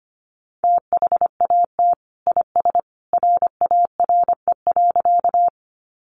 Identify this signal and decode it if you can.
Morse code, heard as a keyed tone, about 25 wpm, 700 Hz. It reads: THAT IS RARE.
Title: Sentences from Top 1100 Words 25wpm